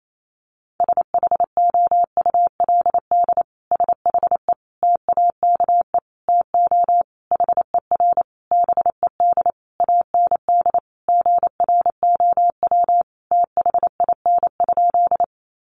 Morse code, heard as a keyed tone, about 28 wpm, 700 Hz. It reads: SHOULD SHE TAKE TO HER BED AND GROW THIN?
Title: Sentences from Top 1200 Words 28wpm